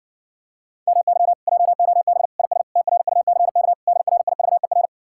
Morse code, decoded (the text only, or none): MY JOB IS TRUCK DRIVER